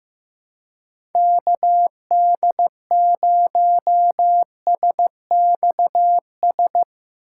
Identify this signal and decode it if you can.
Morse code, heard as a keyed tone, about 15 wpm, 700 Hz. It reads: KD0SXS
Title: USA Call Signs - 6 Characters - 4x Character Spacing - 15wpm